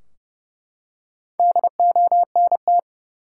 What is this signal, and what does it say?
Morse code, heard as a keyed tone, about 30 wpm, 700 Hz. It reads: DONT
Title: Top 180 QSO Elements 30wpm